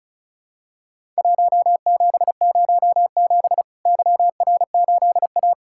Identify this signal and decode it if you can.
Morse code, heard as a keyed tone, about 36 wpm, 700 Hz. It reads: 1707 YR8A
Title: Worked All Europe DX Contest - QTC Exchange - 35wpm